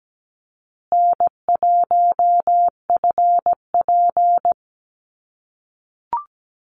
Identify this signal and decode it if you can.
Morse code, heard as a keyed tone, about 17 wpm, 700 Hz. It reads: N1FP
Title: USA Call Signs Repeated 3x 17wpm